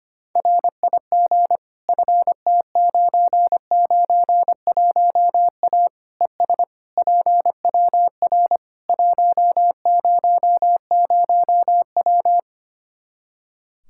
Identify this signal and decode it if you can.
Morse code, heard as a keyed tone, about 25 wpm, 700 Hz. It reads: RIG FT991A ES PWR 100W
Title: Practice QSO - Exchange 2 - Info - 25wpm